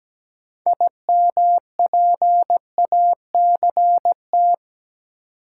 Morse code, heard as a keyed tone, about 17 wpm, 700 Hz. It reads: IMPACT
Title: Sentences from Top 1000 Words - Review Part 6 - New Word Drill - 17wpm